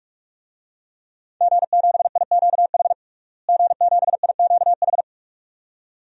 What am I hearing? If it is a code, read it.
Morse code, heard as a keyed tone, about 46 wpm, 700 Hz. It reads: G7IQH G7IQH
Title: DXCC-100 Call Signs - 5 Characters - Repeated 2x - 45wpm